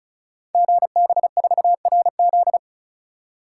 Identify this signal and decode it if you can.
Morse code, heard as a keyed tone, about 36 wpm, 700 Hz. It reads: GB4RZ
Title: Code Groups - 5 Letter:Numbers - 35wpm